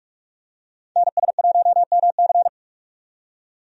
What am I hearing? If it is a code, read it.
Morse code, heard as a keyed tone, about 41 wpm, 700 Hz. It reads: NS1MC
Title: USA Call Signs - 5 Characters - 45wpm